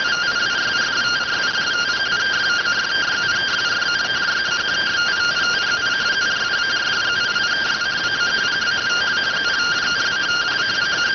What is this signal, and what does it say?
Two-tone teleprinter signal, about 170 Hz shift.